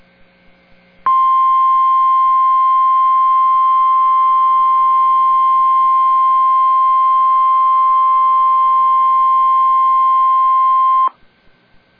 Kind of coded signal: Emergency Alert System (SAME) tones